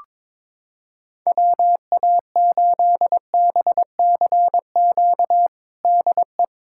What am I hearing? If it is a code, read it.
Morse code, heard as a keyed tone, about 22 wpm, 700 Hz. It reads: WA8BCQ DE
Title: USA Callsign DE Callsign - 6 Characters - 22wpm